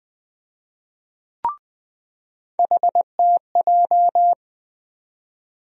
Morse code, heard as a keyed tone, about 20 wpm, 700 Hz. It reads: HTJ
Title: USA Call Sign Postfixes - 3 Letters 20wpm